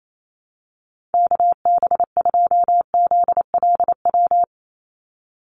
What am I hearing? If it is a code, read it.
Morse code, heard as a keyed tone, about 28 wpm, 700 Hz. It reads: KB2ZLW
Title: USA Call Signs 28wpm